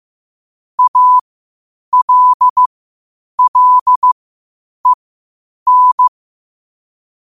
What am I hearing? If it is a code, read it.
Morse code, heard as a keyed tone, about 15 wpm, spaced as for 8 wpm, 1000 Hz. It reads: ALLEN
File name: qotdfiles / 8wpm